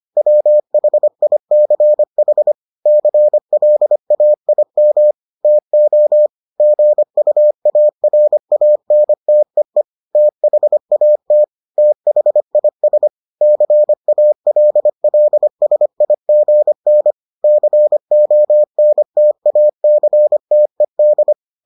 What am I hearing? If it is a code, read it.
Morse code, heard as a keyed tone, about 25 wpm, 600 Hz. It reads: WHICH CLAIM TO GUARANTEE THAT THIS CALLSIGN CONTACTED